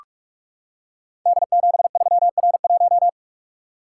Morse code, heard as a keyed tone, about 44 wpm, 700 Hz. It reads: D73R1
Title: Code Groups - 5 Letter:Numbers - 45wpm